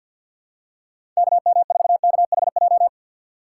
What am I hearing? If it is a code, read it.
Morse code, heard as a keyed tone, about 51 wpm, 700 Hz. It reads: KM4KHJ
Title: USA Call Signs - 6 Characters - 4x Character Spacing - 50wpm